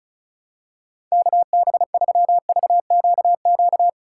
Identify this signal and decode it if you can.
Morse code, heard as a keyed tone, about 36 wpm, 700 Hz. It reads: KB3VQQ
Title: USA Call Signs - 2x Character Spacing - 35wpm